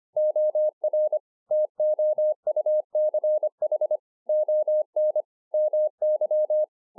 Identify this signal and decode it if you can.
Morse code, heard as a keyed tone, about 25 wpm, 610 Hz. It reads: OR TOUCH ON MY